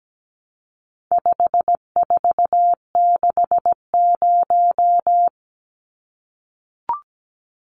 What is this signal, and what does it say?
Morse code, heard as a keyed tone, about 17 wpm, 700 Hz. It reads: 5460
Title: Numbers - 4-Digit 17wpm